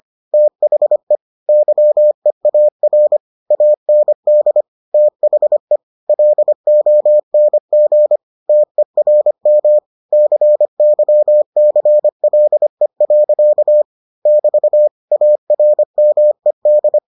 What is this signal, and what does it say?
Morse code, heard as a keyed tone, about 25 wpm, 600 Hz. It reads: THE YEAR AND THE LONG TERM CYCLE. <BT> ARMED